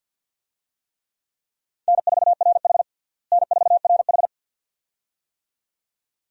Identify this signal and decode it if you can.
Morse code, heard as a keyed tone, about 52 wpm, 700 Hz. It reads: N4RH N4RH
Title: USA Call Signs - 4 Characters - Repeated 2x - 50wpm